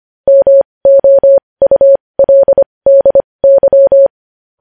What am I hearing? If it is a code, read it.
Morse code, heard as a keyed tone, about 25 wpm, spaced as for 20 wpm, 550 Hz. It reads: MOULDY
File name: podcast / News headlines-1x-25x20